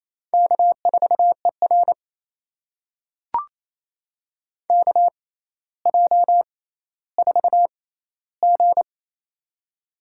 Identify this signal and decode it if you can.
Morse code, heard as a keyed tone, about 28 wpm, spaced as for 14 wpm, 700 Hz. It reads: K4ER KJ4G
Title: USA Call Signs - 4 Characters - 6x Character Spacing - 28wpm